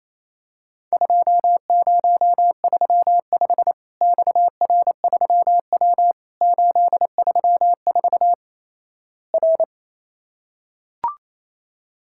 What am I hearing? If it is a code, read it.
Morse code, heard as a keyed tone, about 27 wpm, 700 Hz. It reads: 2035 XR3W 834 R
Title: Worked All Europe DX Contest - QTC Exchange - 28wpm